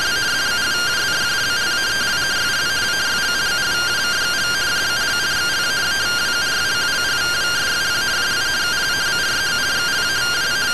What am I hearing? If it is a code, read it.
Two-tone teleprinter signal, about 170 Hz shift.